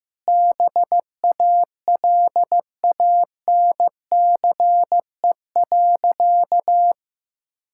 Morse code, heard as a keyed tone, about 15 wpm, 700 Hz. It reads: BALANCE.
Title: Sentences from Top 1000 Words - Review Part 6 - 15wpm